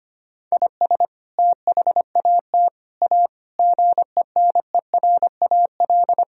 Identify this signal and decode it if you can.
Morse code, heard as a keyed tone, about 25 wpm, 700 Hz. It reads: IS THAT A GENERAL
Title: Sentences from Top 1000 Words - Review Part 3 - 25wpm